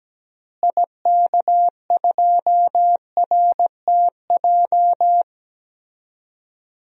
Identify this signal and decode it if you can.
Morse code, heard as a keyed tone, about 17 wpm, 700 Hz. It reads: IK2RTJ
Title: DXCC-100 Call Signs - 6 Characters - Repeated 2x - 17wpm